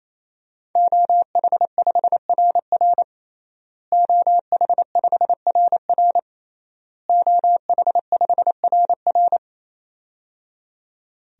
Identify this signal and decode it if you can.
Morse code, heard as a keyed tone, about 28 wpm, 700 Hz. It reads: OH5RR OH5RR OH5RR
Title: DXCC-100 Call Signs - 5 Characters - Repeated 3x - 28wpm